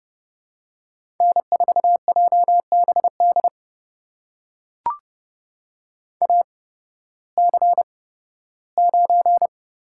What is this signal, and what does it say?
Morse code, heard as a keyed tone, about 30 wpm, spaced as for 16 wpm, 700 Hz. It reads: N4JBD AC9
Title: USA Call Signs - 5 Characters - 8x Character Spacing - 30wpm